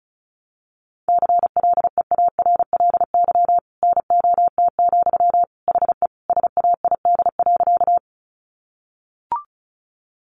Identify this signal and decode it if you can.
Morse code, heard as a keyed tone, about 34 wpm, 700 Hz. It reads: CLEARLY NOT, HE SAID.
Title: Sentences from Top 1300 Words 35wpm